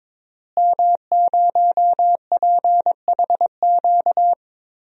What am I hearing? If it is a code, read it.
Morse code, heard as a keyed tone, about 22 wpm, 700 Hz. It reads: M0PHQ
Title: DXCC-100 Call Signs - 5 Characters - Repeated 3x - 22wpm